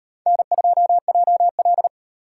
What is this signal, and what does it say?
Morse code, heard as a keyed tone, about 38 wpm, 700 Hz. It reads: N2JL